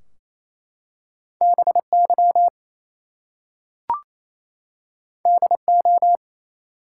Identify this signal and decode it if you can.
Morse code, heard as a keyed tone, about 27 wpm, 700 Hz. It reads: BY DO
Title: Two Letter Words - 28wpm